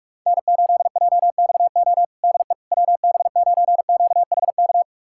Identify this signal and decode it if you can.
Morse code, heard as a keyed tone, about 46 wpm, 700 Hz. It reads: N8JXO DE WB9QHK
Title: USA Callsign DE Callsign 45wpm